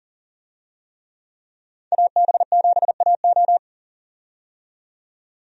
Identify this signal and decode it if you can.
Morse code, heard as a keyed tone, about 37 wpm, 700 Hz. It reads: AB7AO
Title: USA Call Signs - 5 Characters - 40wpm